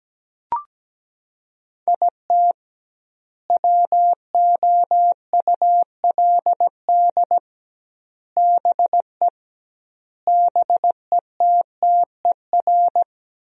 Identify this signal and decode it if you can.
Morse code, heard as a keyed tone, about 17 wpm, 700 Hz. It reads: IT WOULD BE BETTER
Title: 2x Word-Spacing - Sentences from Top 400 Words - 17wpm